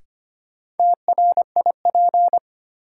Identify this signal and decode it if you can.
Morse code, heard as a keyed tone, about 25 wpm, 700 Hz. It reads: TRIP
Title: Sentences from Top 1000 Words - Review Part 4 - New Word Drill - 25wpm